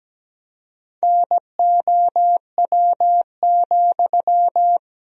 Morse code, heard as a keyed tone, about 17 wpm, 700 Hz. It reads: NOW,
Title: 3x Word-Spacing - Sentences from Top 200 Words - 17wpm